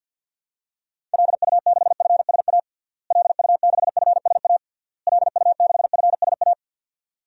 Morse code, heard as a keyed tone, about 51 wpm, 700 Hz. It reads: LU6FSA LU6FSA LU6FSA
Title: DXCC-100 Call Signs - 6 Characters - Repeated 3x - 50wpm